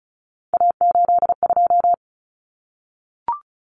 Morse code, heard as a keyed tone, about 34 wpm, 700 Hz. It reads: A82
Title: Common Cut Numbers - 3-Digit 35wpm